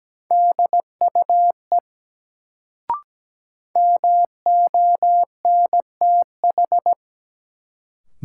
Morse code, heard as a keyed tone, about 17 wpm, 700 Hz. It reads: DUE MONTH
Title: Sets of 5 Words Encore - New Word Drill - 17wpm